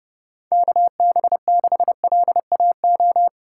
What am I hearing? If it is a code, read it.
Morse code, heard as a keyed tone, about 29 wpm, 700 Hz. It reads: KB6LAO
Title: USA Call Signs - 6 Characters - 3x Character Spacing - 30wpm